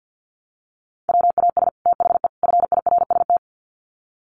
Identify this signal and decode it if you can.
Morse code, heard as a keyed tone, about 50 wpm, 700 Hz. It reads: WAS THE FIRST
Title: Sets of 3 Words 50wpm